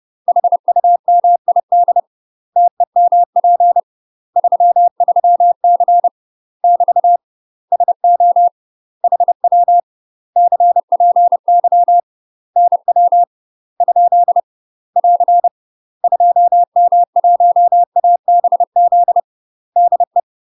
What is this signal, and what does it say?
Morse code, heard as a keyed tone, about 30 wpm, 700 Hz. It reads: HUMID TEMP 33C <BT> SO HW CPY NW ? <AR> 2M1ABZ DE
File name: UK MASTER QSOs TRAINING / 30WPM / UK MASTER QSOs 0045